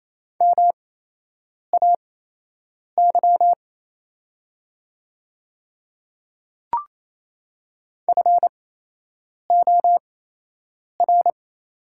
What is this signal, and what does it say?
Morse code, heard as a keyed tone, about 28 wpm, spaced as for 8 wpm, 700 Hz. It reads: MAY FOR
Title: 8x Character Spacing - Top 200 Words - 28wpm